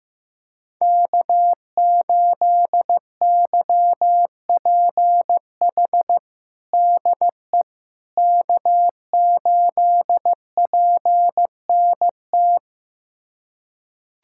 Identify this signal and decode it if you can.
Morse code, heard as a keyed tone, about 15 wpm, 700 Hz. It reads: K8YPH DE K8PNT
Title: USA Callsign DE Callsign - 5 Characters - 15wpm